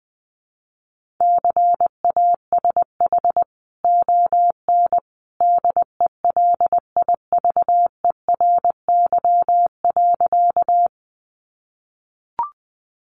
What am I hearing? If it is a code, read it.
Morse code, heard as a keyed tone, about 20 wpm, 700 Hz. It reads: CASH ON DELIVERY.
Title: Sentences from Top 1500 Words 20wpm